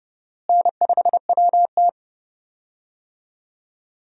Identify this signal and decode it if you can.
Morse code, heard as a keyed tone, about 32 wpm, 700 Hz. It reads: N5WT
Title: USA Call Signs - 4 Characters - Repeated 2x - 30wpm